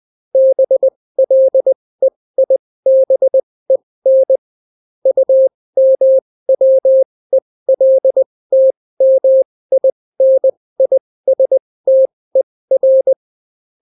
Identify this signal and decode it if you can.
Morse code, heard as a keyed tone, about 20 wpm, spaced as for 16 wpm, 530 Hz. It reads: BLEIBEN UMWELTMINISTER